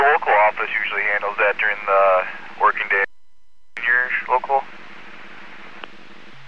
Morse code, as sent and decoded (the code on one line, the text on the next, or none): none